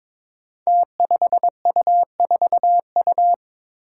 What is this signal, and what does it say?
Morse code, heard as a keyed tone, about 22 wpm, 700 Hz. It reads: T5U4U
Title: Five Challenging Letter-Number Combinations 22wpm